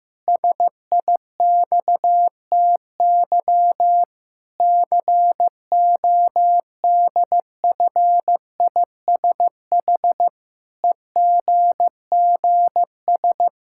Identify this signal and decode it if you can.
Morse code, heard as a keyed tone, about 15 wpm, 700 Hz. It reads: SIXTY CODFISH EGGS
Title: Pangram Challenge v2 - 15wpm